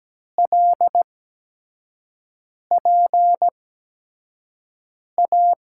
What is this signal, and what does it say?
Morse code, heard as a keyed tone, about 17 wpm, spaced as for 5 wpm, 700 Hz. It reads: LPA
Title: USA Call Signs - 6 Characters - 8x Character Spacing - 17wpm